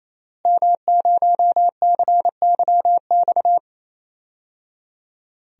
Morse code, heard as a keyed tone, about 28 wpm, 700 Hz. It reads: M0CYX
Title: DXCC-100 Call Signs - 5 Characters - Repeated 3x - 28wpm